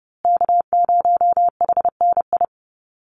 Morse code, heard as a keyed tone, about 30 wpm, 700 Hz. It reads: K0HNI
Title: USA Call Signs Repeated 3x 30wpm